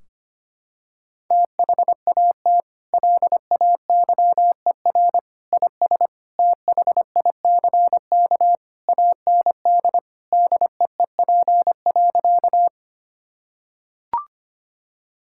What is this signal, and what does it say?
Morse code, heard as a keyed tone, about 25 wpm, 700 Hz. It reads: THAT LAYER IS THICK AND DEEP.
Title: Sentences from Top 1600 Words 25wpm